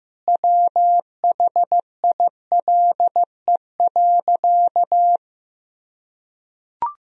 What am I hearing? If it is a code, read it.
Morse code, heard as a keyed tone, about 15 wpm, 700 Hz. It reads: WHILE.
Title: Sentences from Top 1000 Words - Review Part 1 - 15wpm